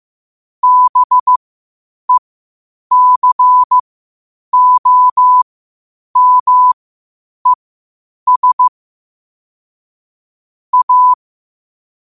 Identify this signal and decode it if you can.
Morse code, heard as a keyed tone, about 15 wpm, spaced as for 8 wpm, 1000 Hz. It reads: BECOMES A
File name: qotdfiles / 8wpm